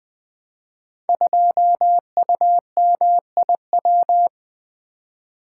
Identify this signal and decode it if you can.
Morse code, heard as a keyed tone, about 20 wpm, 700 Hz. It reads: 2UMIW
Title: Code Groups - 5 Letter:Numbers - 20wpm